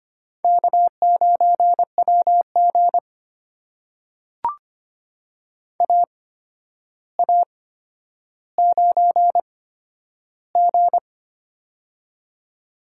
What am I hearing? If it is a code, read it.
Morse code, heard as a keyed tone, about 25 wpm, spaced as for 11 wpm, 700 Hz. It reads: K9WG AA9G
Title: USA Call Signs - 4 Characters - 8x Character Spacing - 25wpm